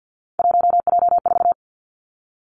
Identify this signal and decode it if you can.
Morse code, heard as a keyed tone, about 50 wpm, 700 Hz. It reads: 1J4
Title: Three Challenging Letter-Number Combinations 50wpm